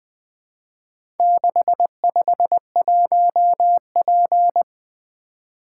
Morse code, heard as a keyed tone, about 23 wpm, 700 Hz. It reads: 651P
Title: Speed-Racing - Four Letter-Number Combinations - 20wpm